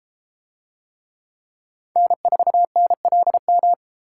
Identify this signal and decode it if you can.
Morse code, heard as a keyed tone, about 32 wpm, 700 Hz. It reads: N4NLM